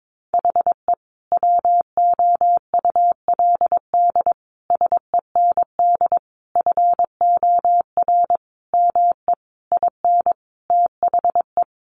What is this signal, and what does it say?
Morse code, heard as a keyed tone, about 22 wpm, 700 Hz. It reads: HE WOULD SEND FOR ME IN THE
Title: Sentences from Top 1500 Words 22wpm